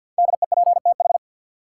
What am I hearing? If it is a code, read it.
Morse code, heard as a keyed tone, about 52 wpm, 700 Hz. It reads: DEPTH
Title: Sentences from Top 1200 Words - New Word Drill - 50wpm